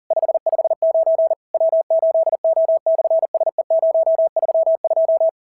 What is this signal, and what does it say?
Morse code, heard as a keyed tone, about 40 wpm, 660 Hz. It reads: 559 W8O/SE032
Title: SOTA QSO Practice 40wpm